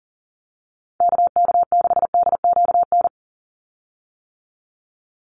Morse code, heard as a keyed tone, about 40 wpm, 700 Hz. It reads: KK6DQN
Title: USA Call Signs 40wpm